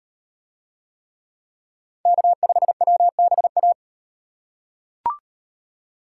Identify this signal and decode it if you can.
Morse code, heard as a keyed tone, about 36 wpm, 700 Hz. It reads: K5WBA